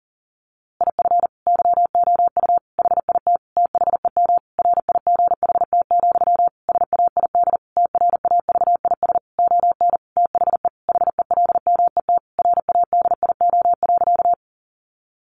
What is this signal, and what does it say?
Morse code, heard as a keyed tone, about 40 wpm, 700 Hz. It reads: IF YOU HIT THEM RIGHT, SAID TRAVIS ON THE HELMET RADIO.
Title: A Sound Of Thunder 40wpm